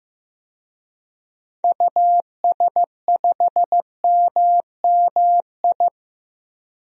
Morse code, heard as a keyed tone, about 15 wpm, 700 Hz. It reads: US5MMI
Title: DXCC-100 Call Signs - 6 Characters - 15wpm